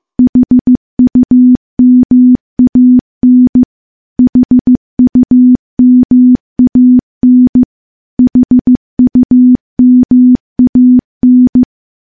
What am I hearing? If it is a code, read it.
Morse code, heard as a keyed tone, about 16 wpm, 250 Hz. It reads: HUMAN HUMAN HUMAN
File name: morse-training-15WPM